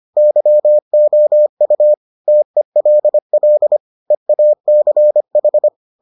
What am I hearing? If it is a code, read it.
Morse code, heard as a keyed tone, about 25 wpm, 600 Hz. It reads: YOU TELL EACH